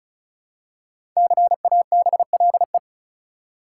Morse code, heard as a keyed tone, about 34 wpm, 700 Hz. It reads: CABLE
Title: Five Letter Words - 35wpm